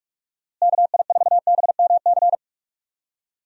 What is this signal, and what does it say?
Morse code, heard as a keyed tone, about 46 wpm, 700 Hz. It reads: KI4BMC
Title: USA Call Signs - 6 Characters - Repeated 3x - 45wpm